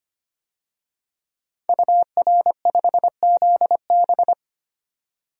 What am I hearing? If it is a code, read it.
Morse code, heard as a keyed tone, about 25 wpm, 700 Hz. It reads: UR5ZB
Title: DXCC-100 Call Signs - 5 Characters - 25wpm